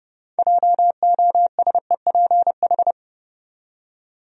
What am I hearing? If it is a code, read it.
Morse code, heard as a keyed tone, about 30 wpm, 700 Hz. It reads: JOSEPH
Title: USA Common Names - 30wpm